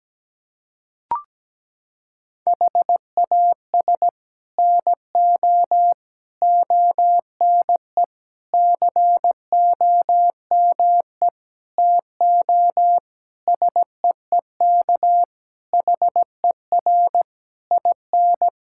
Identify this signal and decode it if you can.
Morse code, heard as a keyed tone, about 17 wpm, 700 Hz. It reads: HAS NO ONE COME TO SEEK HER IN
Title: Sentences from Top 1600 Words 17wpm